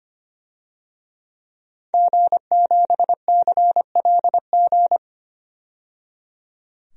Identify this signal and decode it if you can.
Morse code, heard as a keyed tone, about 25 wpm, 700 Hz. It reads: G7CLG
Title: DXCC-100 Call Signs - 5 Characters - 25wpm